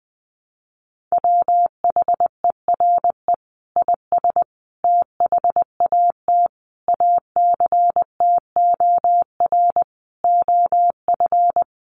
Morse code, heard as a keyed tone, about 20 wpm, 700 Hz. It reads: WHERE IS THAT ACTOR OF
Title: Sentences from Top 1500 Words 20wpm